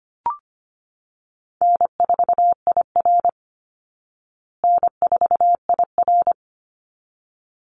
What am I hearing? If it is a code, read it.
Morse code, heard as a keyed tone, about 25 wpm, 700 Hz. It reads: N4IR N4IR
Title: Easy Call Signs Repeated 2x - 25wpm